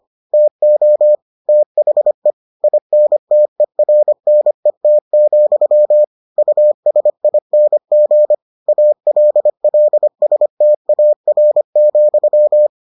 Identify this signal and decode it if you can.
Morse code, heard as a keyed tone, about 25 wpm, 600 Hz. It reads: TO THE INTERNET, USING ALLSTAR,